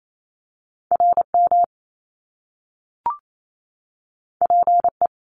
Morse code, heard as a keyed tone, about 28 wpm, 700 Hz. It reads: RM PE
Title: Top Two Letter Combinations in Words 28wpm